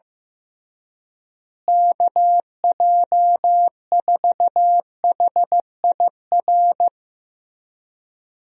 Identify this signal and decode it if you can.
Morse code, heard as a keyed tone, about 16 wpm, 700 Hz. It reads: KJ4HIR
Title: USA Call Signs - 6 Characters - Repeated 3x - 15wpm